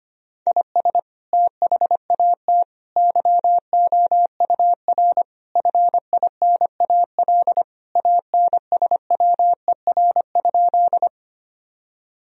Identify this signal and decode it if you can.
Morse code, heard as a keyed tone, about 25 wpm, 700 Hz. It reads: IS THAT YOUR FINAL ANSWER?
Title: Sentences from Top 1000 Words - Review Part 2 - 25wpm